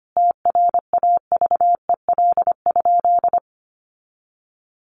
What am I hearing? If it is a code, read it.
Morse code, heard as a keyed tone, about 25 wpm, 700 Hz. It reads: TRAVEL?
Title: Sentences from Top 1400 Words 25wpm